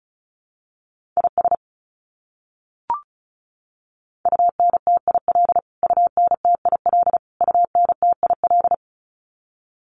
Morse code, heard as a keyed tone, about 34 wpm, 700 Hz. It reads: IS UNTIL UNTIL UNTIL
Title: Top 500 Words Repeated 35wpm